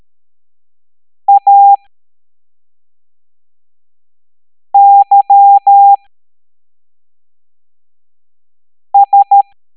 Morse code, heard as a keyed tone, about 13 wpm, 800 Hz. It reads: A Y S
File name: hamdata / L05-EX01-SVY